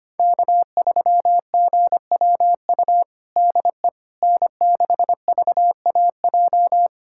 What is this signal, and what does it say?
Morse code, heard as a keyed tone, about 25 wpm, 700 Hz. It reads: K3GWU DE N6VAJ
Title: USA Callsign DE Callsign - 5 Characters - 25wpm